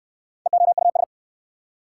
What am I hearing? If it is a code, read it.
Morse code, heard as a keyed tone, about 55 wpm, 700 Hz. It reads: EOUS
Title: Speed-Racing Prefixes and Suffixes 45wpm